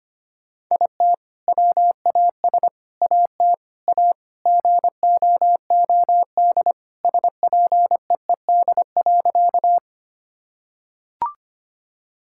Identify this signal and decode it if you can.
Morse code, heard as a keyed tone, about 25 wpm, 700 Hz. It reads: IT WAS AT A GOOD SPEED.
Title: Sentences from Top 1000 Words - Review Part 5 - 25wpm